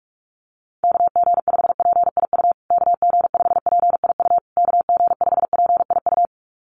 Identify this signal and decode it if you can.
Morse code, heard as a keyed tone, about 44 wpm, 700 Hz. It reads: KG5PIU KG5PIU KG5PIU
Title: USA Call Signs Repeated 3x 45wpm